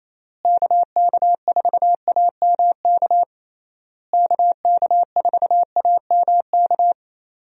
Morse code, heard as a keyed tone, about 28 wpm, 700 Hz. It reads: KK4AMK KK4AMK
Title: USA Call Signs - 6 Characters - Repeated 3x - 28wpm